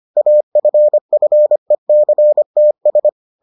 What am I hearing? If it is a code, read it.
Morse code, heard as a keyed tone, about 25 wpm, 600 Hz. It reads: AFFECTS